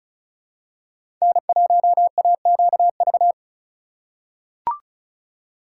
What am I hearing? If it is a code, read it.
Morse code, heard as a keyed tone, about 34 wpm, 700 Hz. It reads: N1AQV
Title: USA Call Signs - 5 Characters - 4x Character Spacing - 35wpm